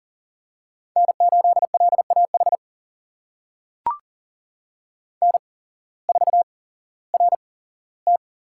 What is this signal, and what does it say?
Morse code, heard as a keyed tone, about 38 wpm, spaced as for 18 wpm, 700 Hz. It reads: N8LAH N4RT
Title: USA Call Signs - 5 Characters - 8x Character Spacing - 40wpm